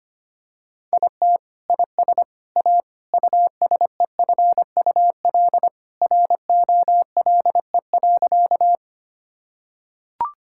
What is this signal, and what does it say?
Morse code, heard as a keyed tone, about 25 wpm, 700 Hz. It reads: IT IS A USEFUL ROLE.
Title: Sentences from Top 1000 Words - Review Part 6 - 25wpm